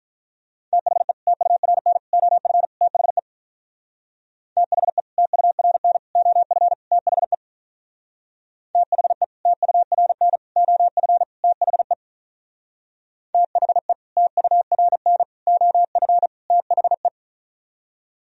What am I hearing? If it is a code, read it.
Morse code, heard as a keyed tone, about 43 wpm, 700 Hz. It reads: THE TURN OF THE THE TURN OF THE THE TURN OF THE THE TURN OF THE
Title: Speed-Racing - Sets of 4 Words 35wpm